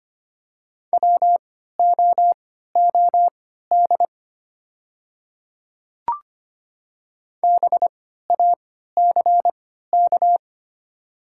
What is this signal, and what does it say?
Morse code, heard as a keyed tone, about 25 wpm, spaced as for 14 wpm, 700 Hz. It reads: WOOD BACK
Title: 3x Character Spacing - Top 300 Words - 25wpm